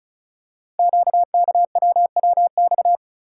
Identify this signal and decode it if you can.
Morse code, heard as a keyed tone, about 36 wpm, 690 Hz. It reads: QKWWX
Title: Code Groups - 5 Letters - 35wpm